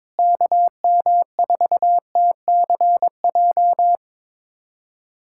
Morse code, heard as a keyed tone, about 22 wpm, 700 Hz. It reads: KM4TCJ
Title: USA Callsign DE Callsign - 6 Characters - 22wpm